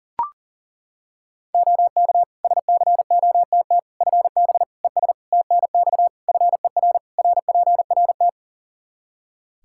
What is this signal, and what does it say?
Morse code, heard as a keyed tone, about 37 wpm, 700 Hz. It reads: OK SCOTT FB ES TNX FER RPRT
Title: Practice QSO - Exchange 2 - Info - 40wpm